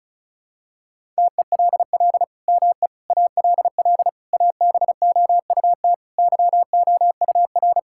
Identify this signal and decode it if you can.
Morse code, heard as a keyed tone, about 36 wpm, 700 Hz. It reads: TELL ME ALL ABOUT YOUR
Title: Sentences from Top 1000 Words - Review Part 6 - 35wpm